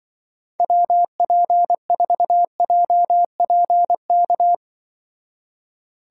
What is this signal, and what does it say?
Morse code, heard as a keyed tone, about 24 wpm, 700 Hz. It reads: WP4JPK
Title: Speed Racing - USA 2x3 Call Signs 20wpm